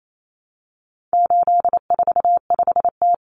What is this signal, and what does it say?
Morse code, heard as a keyed tone, about 28 wpm, 700 Hz. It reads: 845T
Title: Common Cut Numbers - 4-Digit 28wpm